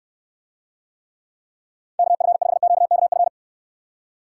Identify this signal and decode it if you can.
Morse code, heard as a keyed tone, about 69 wpm, 700 Hz. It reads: KF5QKF